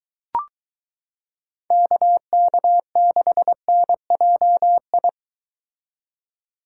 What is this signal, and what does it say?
Morse code, heard as a keyed tone, about 23 wpm, 700 Hz. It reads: KK6NJI